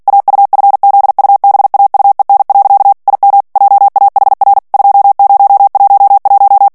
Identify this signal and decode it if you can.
Morse code, heard as a keyed tone, about 48 wpm, 800 Hz. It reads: AUFZUBAUEN. IM JAHR 2011
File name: morse mp3 / ALLEMAND texte clair 2 vitesse 45 mots minute